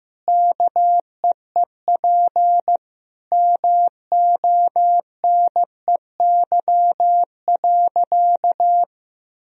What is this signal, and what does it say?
Morse code, heard as a keyed tone, about 15 wpm, 700 Hz. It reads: KEEP MONEY.
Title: Sentences from Top 1000 Words - Review Part 4 - 15wpm